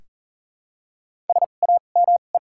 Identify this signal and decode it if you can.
Morse code, heard as a keyed tone, about 40 wpm, spaced as for 29 wpm, 700 Hz. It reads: SAME
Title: 2x Character Spacing - Top 500 Words - 40wpm